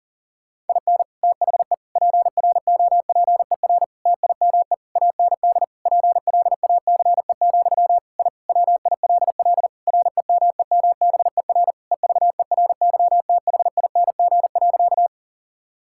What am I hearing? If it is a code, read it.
Morse code, heard as a keyed tone, about 40 wpm, 700 Hz. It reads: IN THE PROPER TIME AND PLACE, I WILL REMEMBER EVERYTHING.
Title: Sentences from Top 1000 Words - Review Part 5 - 40wpm